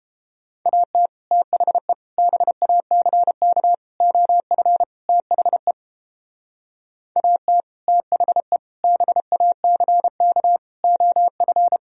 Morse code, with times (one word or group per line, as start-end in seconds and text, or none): 0.7-1.1 AT
1.3-1.9 THE
2.2-3.7 BACK
4.0-4.8 OF
5.1-5.7 THE
7.2-7.6 AT
7.9-8.6 THE
8.8-10.6 BACK
10.8-11.8 OF